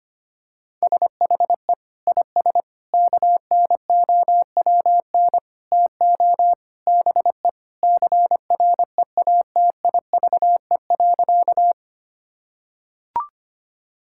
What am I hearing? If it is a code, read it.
Morse code, heard as a keyed tone, about 25 wpm, 700 Hz. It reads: SHE IS KNOWN TO BE CREATIVE.
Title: Sentences from Top 1000 Words - Review Part 5 - 25wpm